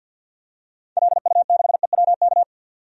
Morse code, heard as a keyed tone, about 45 wpm, 700 Hz. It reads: LU6EWK
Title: DXCC-100 Call Signs - 6 Characters - 50wpm